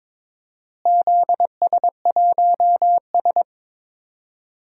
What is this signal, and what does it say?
Morse code, heard as a keyed tone, about 22 wpm, 700 Hz. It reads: ZS1S
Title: DXCC-100 Call Signs - 4 Characters - Repeated 2x - 22wpm